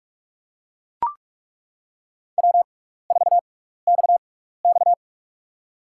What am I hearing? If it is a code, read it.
Morse code, heard as a keyed tone, about 41 wpm, spaced as for 15 wpm, 700 Hz. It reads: W4XX
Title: USA Call Signs - 4 Characters - 6x Character Spacing - 45wpm